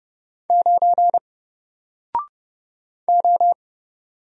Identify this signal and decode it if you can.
Morse code, heard as a keyed tone, about 30 wpm, 700 Hz. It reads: 9 O
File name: Single-Letter-Number-30wpm